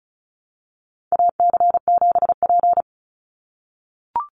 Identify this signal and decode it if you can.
Morse code, heard as a keyed tone, about 34 wpm, 700 Hz. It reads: AC7P
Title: NA SOTA Chaser Callsigns 35wpm